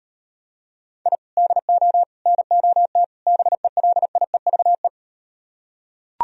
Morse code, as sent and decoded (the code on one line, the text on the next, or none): .. / -.. --- / -. --- - / -... .-.. .. ...- .
I DO NOT BELIEVE